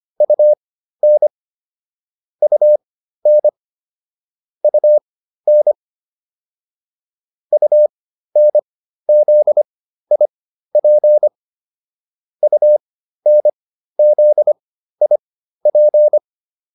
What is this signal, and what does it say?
Morse code, heard as a keyed tone, about 25 wpm, spaced as for 13 wpm, 600 Hz. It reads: UN UN UN UNZIP UNZIP
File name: UN5 13